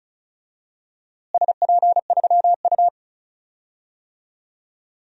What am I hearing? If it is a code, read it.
Morse code, heard as a keyed tone, about 34 wpm, 700 Hz. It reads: SP3U
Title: DXCC-100 Call Signs - 4 Characters - Repeated 3x - 35wpm